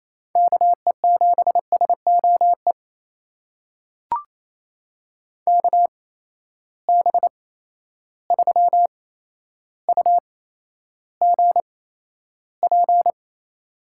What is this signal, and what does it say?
Morse code, heard as a keyed tone, about 28 wpm, spaced as for 12 wpm, 700 Hz. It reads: KE7SOE KB3UGP
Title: USA Call Signs - 6 Characters - 8x Character Spacing - 28wpm